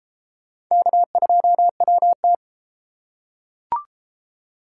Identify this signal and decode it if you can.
Morse code, heard as a keyed tone, about 32 wpm, 700 Hz. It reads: K2WT